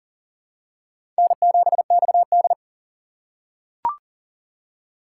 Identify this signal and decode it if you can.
Morse code, heard as a keyed tone, about 40 wpm, 700 Hz. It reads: N7XD
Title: USA Call Signs - 4 Characters - 3x Character Spacing - 40wpm